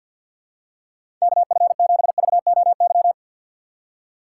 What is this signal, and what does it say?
Morse code, heard as a keyed tone, about 53 wpm, 700 Hz. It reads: KF7VOY
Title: USA Call Signs - 6 Characters - 3x Character Spacing - 50wpm